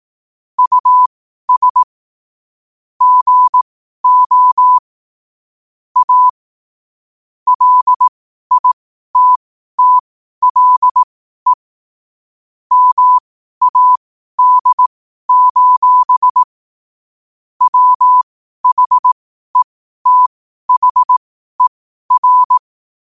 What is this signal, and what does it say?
Morse code, heard as a keyed tone, about 18 wpm, spaced as for 12 wpm, 1000 Hz. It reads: US GO A LITTLE MAD: WHETHER
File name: qotdfiles / 12wpm